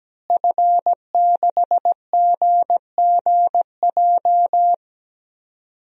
Morse code, heard as a keyed tone, about 17 wpm, 700 Hz. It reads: F6GGJ
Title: DXCC-100 Call Signs - 5 Characters - Repeated 2x - 17wpm